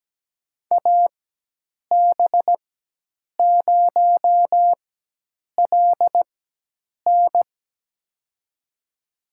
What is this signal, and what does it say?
Morse code, heard as a keyed tone, about 17 wpm, spaced as for 8 wpm, 700 Hz. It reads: AB0LN
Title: USA Call Signs - 5 Characters - 4x Character Spacing - 17wpm